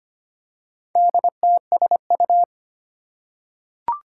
Morse code, heard as a keyed tone, about 25 wpm, 700 Hz. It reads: DTSU
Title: Four Challenging Letter-Number Combinations 25wpm